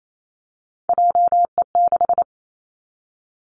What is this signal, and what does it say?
Morse code, heard as a keyed tone, about 28 wpm, 700 Hz. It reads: JE6
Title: Three Challenging Letter-Number Combinations 28wpm